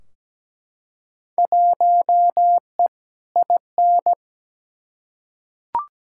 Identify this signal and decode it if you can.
Morse code, heard as a keyed tone, about 17 wpm, 700 Hz. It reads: 1E IN
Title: Field Day Exchanges - 17wpm